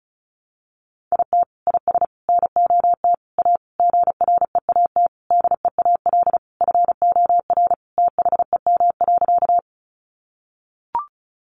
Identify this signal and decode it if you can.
Morse code, heard as a keyed tone, about 34 wpm, 700 Hz. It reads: IT IS NOT A GREAT DEAL FOR THEM.
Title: Sentences from Top 800 Words 35wpm